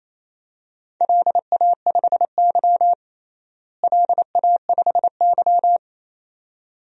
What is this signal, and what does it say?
Morse code, heard as a keyed tone, about 28 wpm, 700 Hz. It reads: LA5Y LA5Y
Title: DXCC-100 Call Signs - 4 Characters - Repeated 2x - 28wpm